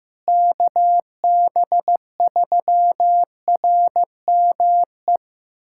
Morse code, heard as a keyed tone, about 15 wpm, 700 Hz. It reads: KB3RME
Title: USA Callsign DE Callsign - 6 Characters - 15wpm